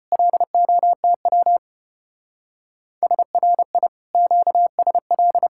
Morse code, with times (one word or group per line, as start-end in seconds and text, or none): 0.1-1.6 LOTW
3.0-3.9 SRI
4.1-5.5 QSL